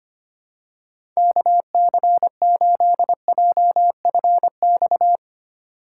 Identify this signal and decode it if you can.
Morse code, heard as a keyed tone, about 25 wpm, 700 Hz. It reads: KC8JFX
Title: USA Call Signs - 8x Character Spacing - 25wpm